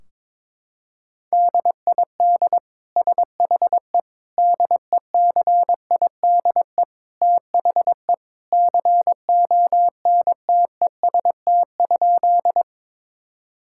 Morse code, heard as a keyed tone, about 22 wpm, 700 Hz. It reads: DID SHE DECIDE THE CONTEST?
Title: Sentences from Top 1600 Words 22wpm